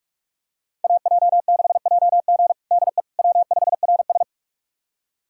Morse code, heard as a keyed tone, about 46 wpm, 700 Hz. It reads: AJ6JG DE W5RS
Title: USA Callsign DE Callsign 45wpm